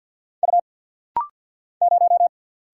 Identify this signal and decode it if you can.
Morse code, heard as a keyed tone, about 46 wpm, 700 Hz. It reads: U 0
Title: Single Letter-Number Alt - Rapid-Fire 50wpm